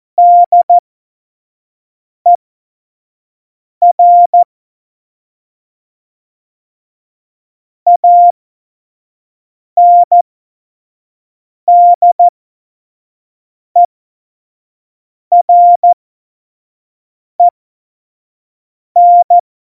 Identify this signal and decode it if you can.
Morse code, heard as a keyed tone, about 14 wpm, spaced as for 5 wpm, 700 Hz. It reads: DER ANDEREN
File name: Witz 31. Oktober 17, Tempo 25